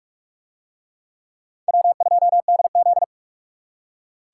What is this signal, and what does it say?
Morse code, heard as a keyed tone, about 46 wpm, 700 Hz. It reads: W2DZ